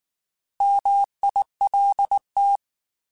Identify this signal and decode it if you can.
Morse code, heard as a keyed tone, about 19 wpm, 800 Hz. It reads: MILT
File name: Alphabetical / Alphabetical mp3s / Ms @18